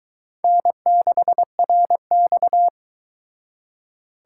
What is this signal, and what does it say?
Morse code, heard as a keyed tone, about 22 wpm, 700 Hz. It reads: N6RX